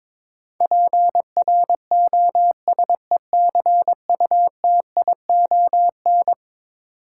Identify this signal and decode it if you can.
Morse code, heard as a keyed tone, about 22 wpm, 700 Hz. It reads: PROSECUTION
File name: practice / 22wpm / Eleven Letter Words - 22wpm